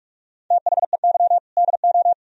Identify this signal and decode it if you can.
Morse code, heard as a keyed tone, about 60 wpm, spaced as for 52 wpm, 700 Hz. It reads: THEY DO
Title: Speed-Racing - Sets of 3 Words 45wpm